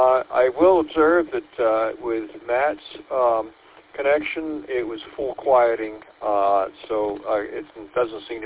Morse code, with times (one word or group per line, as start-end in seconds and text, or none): none